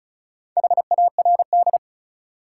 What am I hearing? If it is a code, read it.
Morse code, heard as a keyed tone, about 34 wpm, 700 Hz. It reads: HARD
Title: Sentences from Top 300 Words - New Word Drill - 35wpm